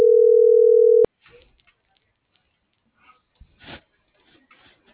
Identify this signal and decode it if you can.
Ringback tone (North America), 440 + 480 Hz.